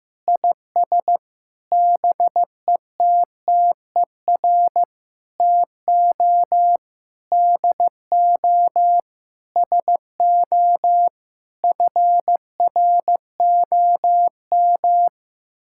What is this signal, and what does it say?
Morse code, heard as a keyed tone, about 15 wpm, 700 Hz. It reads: IS BETTER TO DO SO FROM
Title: Sentences from Top 1600 Words 15wpm